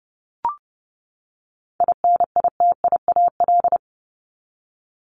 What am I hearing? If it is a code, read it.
Morse code, heard as a keyed tone, about 30 wpm, 700 Hz. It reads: INITIAL
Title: Top 1501-1750 Words 30wpm